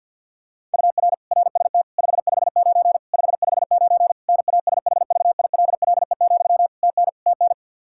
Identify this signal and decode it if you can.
Morse code, heard as a keyed tone, about 50 wpm, 700 Hz. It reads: UR RST 559 559 NASHVILLE, TN TN
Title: Practice QSO - Exchange 1 - Beginning - 50wpm